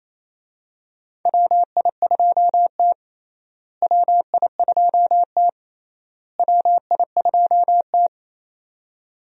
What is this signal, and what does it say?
Morse code, heard as a keyed tone, about 28 wpm, 700 Hz. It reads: WI2T WI2T WI2T
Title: USA Call Signs - 4 Characters - Repeated 3x - 28wpm